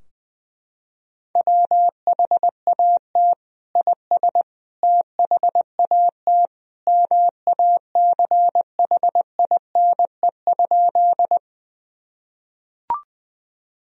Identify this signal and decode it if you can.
Morse code, heard as a keyed tone, about 20 wpm, 700 Hz. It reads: WHAT IS THAT MACHINE?
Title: Sentences from Top 1000 Words - Review Part 2 - 20wpm